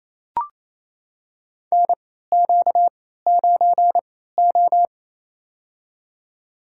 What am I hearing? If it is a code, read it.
Morse code, heard as a keyed tone, about 28 wpm, spaced as for 16 wpm, 700 Hz. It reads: NQ9O
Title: USA Call Signs - 4 Characters - 3x Character Spacing - 28wpm